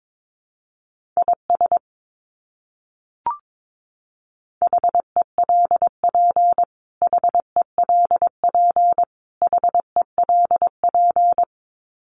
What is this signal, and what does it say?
Morse code, heard as a keyed tone, about 22 wpm, 700 Hz. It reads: IS HELP HELP HELP
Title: Top 200 Words Repeated 22wpm